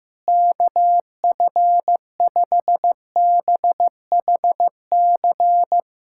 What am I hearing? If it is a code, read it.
Morse code, heard as a keyed tone, about 16 wpm, 700 Hz. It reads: KF5BHC
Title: USA Callsign DE Callsign 15wpm